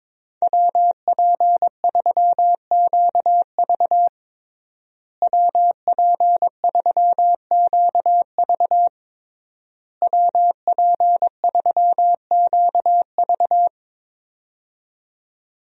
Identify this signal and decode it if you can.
Morse code, heard as a keyed tone, about 22 wpm, 700 Hz. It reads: WP3QV WP3QV WP3QV
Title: USA Call Signs - 5 Characters - Repeated 3x - 22wpm